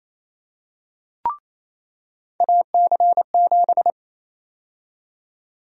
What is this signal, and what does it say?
Morse code, heard as a keyed tone, about 27 wpm, 700 Hz. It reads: AC7
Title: USA Call Sign Prefixes - 28wpm